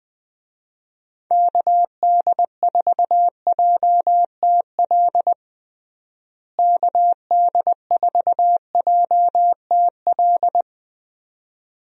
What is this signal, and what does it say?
Morse code, heard as a keyed tone, about 21 wpm, 700 Hz. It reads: KD4JTL KD4JTL
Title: USA Call Signs - 6 Characters - Repeated 2x - 20wpm